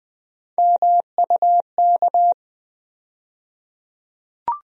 Morse code, heard as a keyed tone, about 20 wpm, 700 Hz. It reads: MUK
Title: USA Call Sign Postfixes - 3 Letters 20wpm